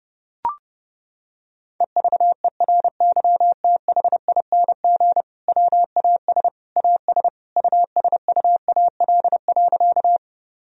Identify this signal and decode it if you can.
Morse code, heard as a keyed tone, about 32 wpm, 700 Hz. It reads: EVERYTHING WAS AS USUAL.
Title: Sentences from Top 1000 Words - Review Part 2 - 30wpm